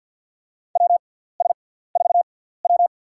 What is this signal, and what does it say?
Morse code, heard as a keyed tone, about 53 wpm, spaced as for 17 wpm, 690 Hz. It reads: WS4W
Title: USA Call Signs - 6x Character Spacing - 50wpm